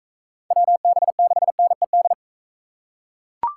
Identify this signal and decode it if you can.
Morse code, heard as a keyed tone, about 43 wpm, 700 Hz. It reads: WB6NED